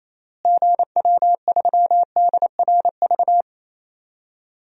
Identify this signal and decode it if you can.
Morse code, heard as a keyed tone, about 28 wpm, 700 Hz. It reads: GW3DRV
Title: DXCC-100 Call Signs - 6 Characters - Repeated 3x - 28wpm